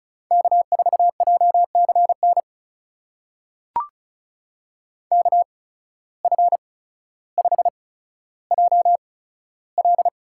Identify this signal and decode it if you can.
Morse code, heard as a keyed tone, about 36 wpm, spaced as for 15 wpm, 700 Hz. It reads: K4JCN KF5JL
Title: USA Call Signs - 5 Characters - 8x Character Spacing - 35wpm